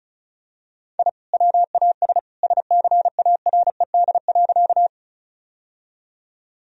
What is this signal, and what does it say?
Morse code, heard as a keyed tone, about 36 wpm, 700 Hz. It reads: I WAS SCARED.
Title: Sentences from Top 1000 Words - Review Part 4 - 35wpm